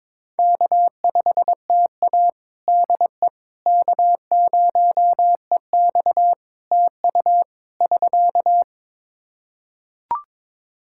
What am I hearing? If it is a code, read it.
Morse code, heard as a keyed tone, about 21 wpm, 700 Hz. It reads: K5TA DE K0EX TU <SK>
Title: Practice QSO - Exchange 4 - Ending - 22wpm